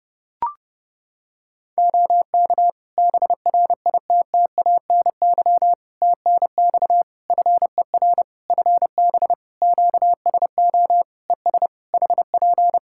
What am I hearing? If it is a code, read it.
Morse code, heard as a keyed tone, about 32 wpm, 700 Hz. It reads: OK BRITTANY TNX FER FB QSO ES HP
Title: K1USN SST QSO Practice 30wpm